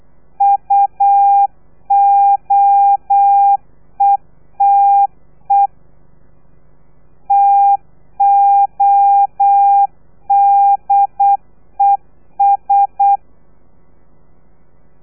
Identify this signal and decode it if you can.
Morse code, heard as a keyed tone, about 8 wpm, 800 Hz. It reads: UOETE TODES
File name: telegrafia / i2tfj / Lez3 35